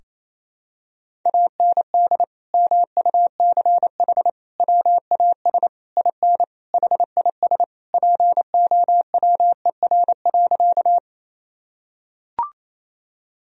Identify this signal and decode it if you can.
Morse code, heard as a keyed tone, about 27 wpm, 700 Hz. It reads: AND MUCH WAS IN HIS POWER.
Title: Sentences from Top 1000 Words - Review Part 2 - 28wpm